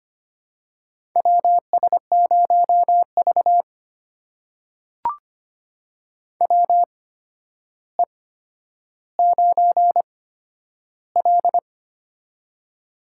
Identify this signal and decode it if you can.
Morse code, heard as a keyed tone, about 25 wpm, spaced as for 11 wpm, 700 Hz. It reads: WS0V WE9L
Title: USA Call Signs - 4 Characters - 8x Character Spacing - 25wpm